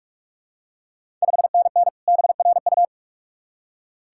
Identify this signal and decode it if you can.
Morse code, heard as a keyed tone, about 46 wpm, 700 Hz. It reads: 5NN BRU
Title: Ontario QSO Party Exchange 45wpm